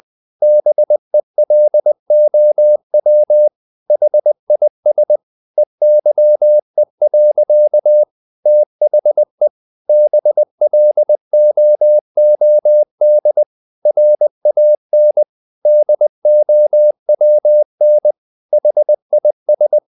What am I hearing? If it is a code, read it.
Morse code, heard as a keyed tone, about 20 wpm, 600 Hz. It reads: BELOW HIS EYE. THE BLOOD RAN DOWN HIS
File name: the old man and the sea / the old man and the sea part 1 20wpm